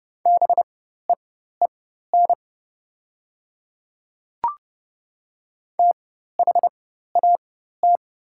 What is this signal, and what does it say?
Morse code, heard as a keyed tone, about 30 wpm, spaced as for 14 wpm, 700 Hz. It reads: BEEN THAT
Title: 4x Character Spacing - Top 100 Words - 30wpm